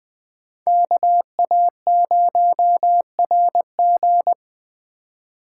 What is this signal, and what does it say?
Morse code, heard as a keyed tone, about 19 wpm, 700 Hz. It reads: KA0RG